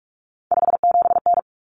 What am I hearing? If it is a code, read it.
Morse code, heard as a keyed tone, about 44 wpm, 700 Hz. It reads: 57N
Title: Common Cut Numbers - 3-Digit 45wpm